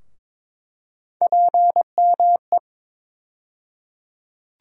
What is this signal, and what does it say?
Morse code, heard as a keyed tone, about 22 wpm, 700 Hz. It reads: PME
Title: USA Call Sign Postfixes - 3 Letters 22wpm